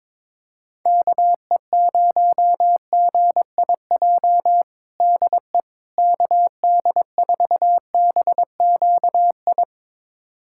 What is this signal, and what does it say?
Morse code, heard as a keyed tone, about 22 wpm, 700 Hz. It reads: KE0GIJ DE KD4BQI
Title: USA Callsign DE Callsign - 6 Characters - 22wpm